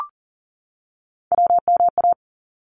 Morse code, heard as a keyed tone, about 40 wpm, 700 Hz. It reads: WMA
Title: ARRL Sections 40wpm